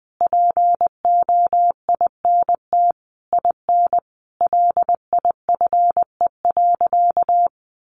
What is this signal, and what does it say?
Morse code, heard as a keyed tone, about 20 wpm, 700 Hz. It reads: POINT IN LIFE.
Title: Sentences from Top 1200 Words 20wpm